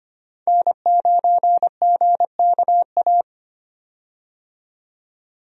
Morse code, heard as a keyed tone, about 25 wpm, 700 Hz. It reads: N9GKA
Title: USA Call Signs - 5 Characters - 25wpm